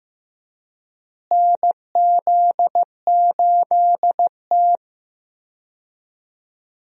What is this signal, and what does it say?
Morse code, heard as a keyed tone, about 15 wpm, 700 Hz. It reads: NZ8T
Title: USA Call Signs - 4 Characters - Repeated 3x - 15wpm